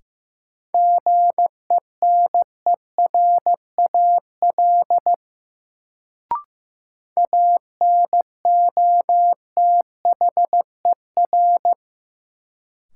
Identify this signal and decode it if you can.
Morse code, heard as a keyed tone, about 15 wpm, 700 Hz. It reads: GENERAL ANOTHER
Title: Sentences from Top 600 Words - New Word Drill - 15wpm